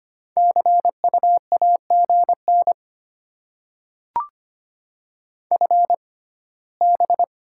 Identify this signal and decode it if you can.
Morse code, heard as a keyed tone, about 25 wpm, spaced as for 17 wpm, 700 Hz. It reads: CU AGN FB
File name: Top 20 QSO Elements - 6x Character Spacing - 25wpm